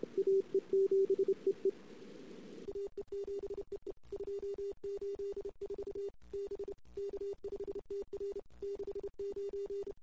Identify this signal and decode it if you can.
Morse code, heard as a keyed tone, about 29 wpm, 410 Hz. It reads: AE7EE AE7EE 284 B K5TR 69
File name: CW / 2008-ss-cw-k5tr-op-n5rz -23:00